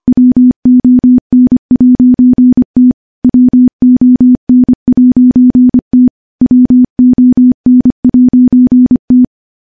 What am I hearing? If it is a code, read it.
Morse code, heard as a keyed tone, about 25 wpm, 250 Hz. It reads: WON'T WON'T WON'T
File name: morse / morse-training-25WPM